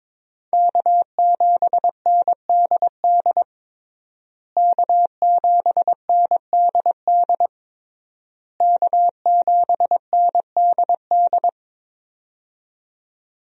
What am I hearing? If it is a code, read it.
Morse code, heard as a keyed tone, about 22 wpm, 700 Hz. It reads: K7NDD K7NDD K7NDD
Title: USA Call Signs - 5 Characters - Repeated 3x - 22wpm